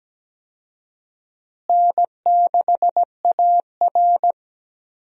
Morse code, heard as a keyed tone, about 17 wpm, 700 Hz. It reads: N6AR
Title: USA Call Signs - 4 Characters - Repeated 2x - 17wpm